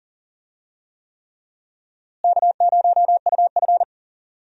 Morse code, heard as a keyed tone, about 40 wpm, 700 Hz. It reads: K0UF
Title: USA Call Signs - 4 Characters - Repeated 3x - 40wpm